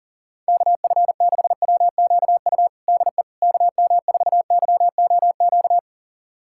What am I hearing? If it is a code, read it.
Morse code, heard as a keyed tone, about 40 wpm, 700 Hz. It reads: KF6WQU DE KM4YOQ
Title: USA Callsign DE Callsign - 6 Characters - 40wpm